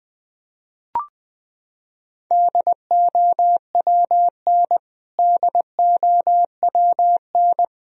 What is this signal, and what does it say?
Morse code, heard as a keyed tone, about 20 wpm, 700 Hz. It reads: DOWN DOWN
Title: Top 100 Words Repeated 20wpm